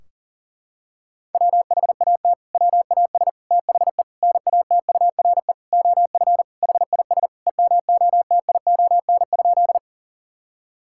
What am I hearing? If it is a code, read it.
Morse code, heard as a keyed tone, about 39 wpm, 700 Hz. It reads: WHAT WAS THE NATURE OF HIS EMOTION?
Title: Sentences from Top 1600 Words 40wpm